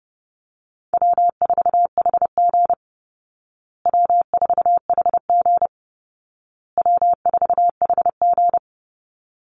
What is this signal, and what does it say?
Morse code, heard as a keyed tone, about 30 wpm, 700 Hz. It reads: W4HG W4HG W4HG
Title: Easy Call Signs Repeated 3x - 30wpm